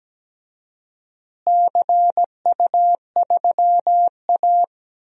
Morse code, heard as a keyed tone, about 17 wpm, 700 Hz. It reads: CU3A
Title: DXCC-100 Call Signs - 4 Characters - Repeated 2x - 17wpm